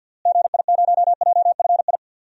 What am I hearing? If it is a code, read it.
Morse code, heard as a keyed tone, about 50 wpm, 700 Hz. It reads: GI0JFI
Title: DXCC-100 Call Signs - 6 Characters - Repeated 3x - 50wpm